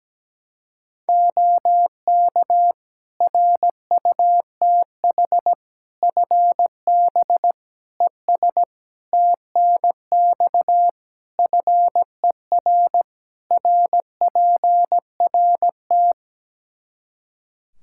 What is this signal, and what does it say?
Morse code, heard as a keyed tone, about 17 wpm, 700 Hz. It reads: OK RUTH FB ES TNX FER RPRT
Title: Practice QSO - Exchange 2 - Info - 17wpm